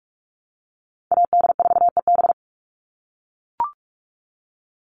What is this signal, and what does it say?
Morse code, heard as a keyed tone, about 44 wpm, 700 Hz. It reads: AD4EB
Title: Call Signs - 45wpm